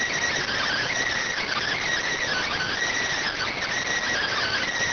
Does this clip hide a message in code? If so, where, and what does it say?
Yes: Emergency Alert System (SAME) tones from 0.5 to 4.7 s, heard as harsh buzzing bursts.